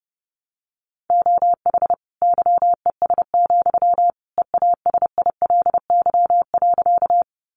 Morse code, heard as a keyed tone, about 30 wpm, 700 Hz. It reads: OH YES, EASILY.
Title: Sentences from Top 900 Words 30wpm